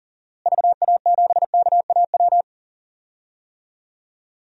Morse code, heard as a keyed tone, about 40 wpm, 700 Hz. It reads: VA7KAW
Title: DXCC-100 Call Signs - 6 Characters - Repeated 2x - 40wpm